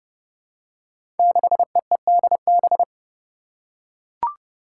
Morse code, heard as a keyed tone, about 30 wpm, 700 Hz. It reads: 6EEDB
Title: Five Challenging Letter-Number Combinations 30wpm